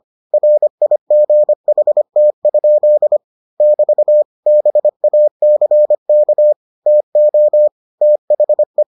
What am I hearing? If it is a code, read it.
Morse code, heard as a keyed tone, about 25 wpm, 600 Hz. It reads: RIGHT? <BT> BACK TO THE